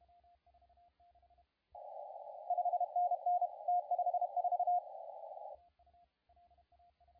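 Morse code, heard as a keyed tone, about 32 wpm, 720 Hz, with radio noise. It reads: 5NN T54
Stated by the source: CW contest audio recorded at LA4C during the SAC CW contest.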